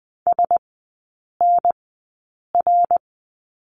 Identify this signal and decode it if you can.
Morse code, heard as a keyed tone, about 20 wpm, spaced as for 8 wpm, 700 Hz. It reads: SNR
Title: 074 Lesson 13e - Letter L - Three Characters 20wpm